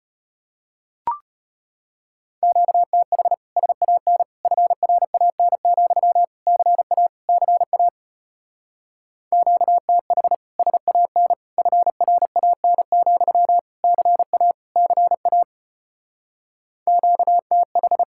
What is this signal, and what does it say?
Morse code, heard as a keyed tone, about 33 wpm, 700 Hz. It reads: QTH SAN FRAN, CA CA QTH SAN FRAN, CA CA QTH